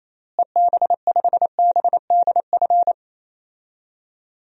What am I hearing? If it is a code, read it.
Morse code, heard as a keyed tone, about 28 wpm, 700 Hz. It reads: EB5BDF
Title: DXCC-100 Call Signs - 6 Characters - Repeated 3x - 28wpm